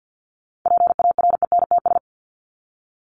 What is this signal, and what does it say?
Morse code, heard as a keyed tone, about 50 wpm, 700 Hz. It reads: PARENTS
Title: ES:S Suffix - Top 100 Words with Base Words 50wpm